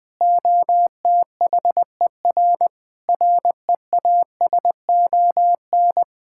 Morse code, heard as a keyed tone, about 20 wpm, 700 Hz. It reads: OTHER REASON
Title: Sentences with 6-Word Phrases - 20wpm